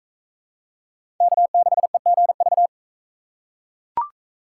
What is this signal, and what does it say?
Morse code, heard as a keyed tone, about 43 wpm, 700 Hz. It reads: K6EGV